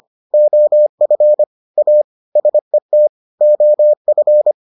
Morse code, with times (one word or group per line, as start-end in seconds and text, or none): 0.3-1.4 OF
1.8-2.0 A
2.4-3.1 SET
3.4-4.5 OF